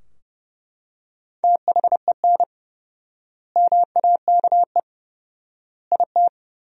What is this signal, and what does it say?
Morse code, heard as a keyed tone, about 30 wpm, 700 Hz. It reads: THEN MAKE IT
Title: 4x Word-Spacing - Sentences from Top 100 Words - 30wpm